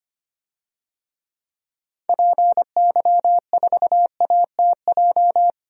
Morse code, heard as a keyed tone, about 25 wpm, 700 Hz. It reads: PY4ATJ
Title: DXCC-100 Call Signs - 6 Characters - 25wpm